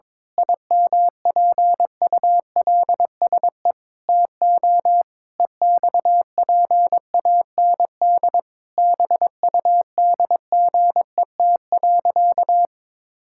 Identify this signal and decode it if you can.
Morse code, heard as a keyed tone, about 22 wpm, 700 Hz. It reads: IMPULSE TO EXPAND BUDGET.
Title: Pangram Challenge v2 - 22wpm